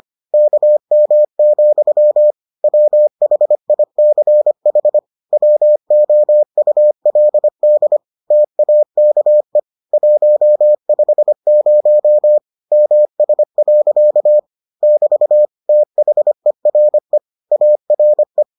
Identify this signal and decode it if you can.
Morse code, heard as a keyed tone, about 25 wpm, 600 Hz. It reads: KM, WHICH WOULD TAKE 150 MS. <BT> THERE ARE